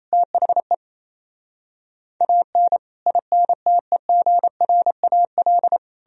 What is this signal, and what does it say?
Morse code, heard as a keyed tone, about 37 wpm, 700 Hz. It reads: THE AN INTEGRAL